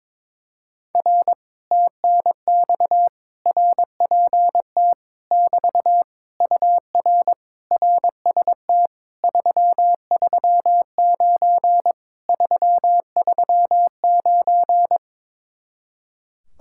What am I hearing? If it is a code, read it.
Morse code, heard as a keyed tone, about 22 wpm, 700 Hz. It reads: R TNX RPT <BT> UR RST 339 339
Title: Speed-Racing Common QSO Phrases - 22wpm